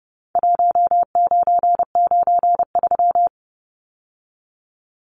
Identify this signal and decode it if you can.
Morse code, heard as a keyed tone, about 30 wpm, 700 Hz. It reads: 1993
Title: Numbers - Contest Style - 4-Digit 30wpm